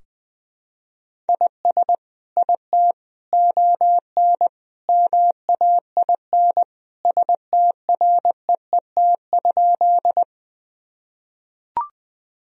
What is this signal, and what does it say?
Morse code, heard as a keyed tone, about 20 wpm, 700 Hz. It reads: IS IT ON MAIN STREET?
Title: Sentences from Top 1000 Words - Review Part 2 - 20wpm